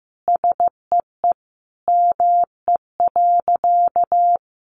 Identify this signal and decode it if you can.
Morse code, heard as a keyed tone, about 15 wpm, 700 Hz. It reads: SEE ME.
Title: Sentences from Top 600 Words 15wpm